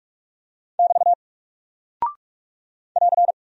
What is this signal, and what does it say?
Morse code, heard as a keyed tone, about 43 wpm, 700 Hz. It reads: <BT> <AR>
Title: Single Character (Letter:Number:Punctuation:Prosign) Alt 45wpm